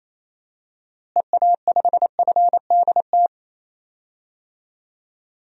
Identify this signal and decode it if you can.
Morse code, heard as a keyed tone, about 28 wpm, 700 Hz. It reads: EA5FDT
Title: DXCC-100 Call Signs - 6 Characters - Repeated 2x - 28wpm